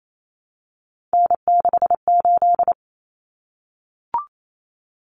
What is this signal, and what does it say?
Morse code, heard as a keyed tone, about 28 wpm, 700 Hz. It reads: N68
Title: Common Cut Numbers - 3-Digit 28wpm